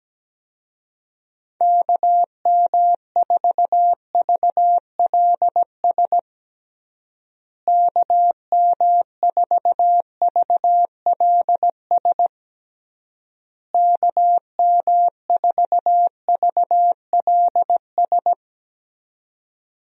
Morse code, heard as a keyed tone, about 17 wpm, 700 Hz. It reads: KM4VLS KM4VLS KM4VLS
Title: USA Call Signs - 6 Characters - Repeated 3x - 17wpm